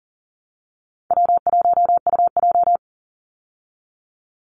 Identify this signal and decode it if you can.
Morse code, heard as a keyed tone, about 40 wpm, 700 Hz. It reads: W1UJ
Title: Call Signs Repeated 3x - 40wpm